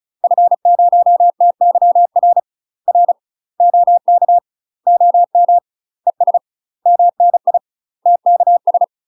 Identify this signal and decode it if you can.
Morse code, heard as a keyed tone, about 34 wpm, 700 Hz. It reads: F0TYR R OK OM ES MNI TKS
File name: UK MASTER QSOs TRAINING / 35WPM / UK MASTER QSOs 0004